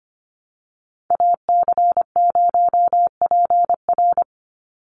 Morse code, heard as a keyed tone, about 25 wpm, 700 Hz. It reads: AC0PR
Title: NA SOTA Activator Callsigns 25wpm